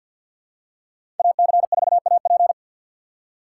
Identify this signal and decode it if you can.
Morse code, heard as a keyed tone, about 46 wpm, 700 Hz. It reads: AC4AP
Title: USA Call Signs - 5 Characters - 6x Character Spacing - 50wpm